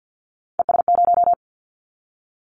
Morse code, heard as a keyed tone, about 50 wpm, 700 Hz. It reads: ES0
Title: Callsign Prefixes for Top 100 DXCC - 50wpm